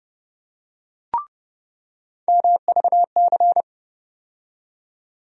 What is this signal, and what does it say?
Morse code, heard as a keyed tone, about 30 wpm, 700 Hz. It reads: MVC
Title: USA Call Sign Postfixes - 3 Letters 30wpm